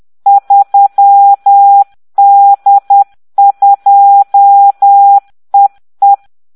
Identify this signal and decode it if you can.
Morse code, heard as a keyed tone, about 10 wpm, 800 Hz. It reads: 3D2EE